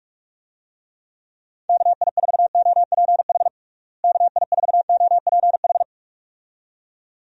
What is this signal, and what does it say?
Morse code, heard as a keyed tone, about 42 wpm, 700 Hz. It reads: KI4OPH KI4OPH
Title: USA Call Signs - 6 Characters - Repeated 2x - 45wpm